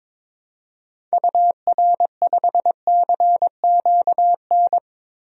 Morse code, heard as a keyed tone, about 22 wpm, 700 Hz. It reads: UR5CQN
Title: DXCC-100 Call Signs - 6 Characters - Repeated 3x - 22wpm